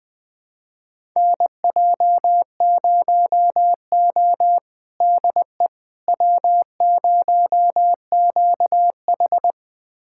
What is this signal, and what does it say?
Morse code, heard as a keyed tone, about 20 wpm, 700 Hz. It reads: NJ0O DE W0QH
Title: USA Callsign DE Callsign - 4 Characters - 20wpm